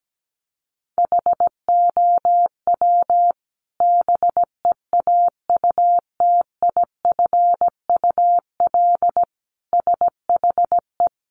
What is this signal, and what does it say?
Morse code, heard as a keyed tone, about 17 wpm, 700 Hz. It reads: HOW BEAUTIFUL SHE
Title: Sentences from Top 900 Words 17wpm